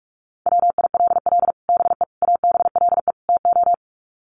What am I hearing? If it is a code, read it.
Morse code, heard as a keyed tone, about 45 wpm, 700 Hz. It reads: WILL BE ABLE TO
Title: Sets of 4 Words 45wpm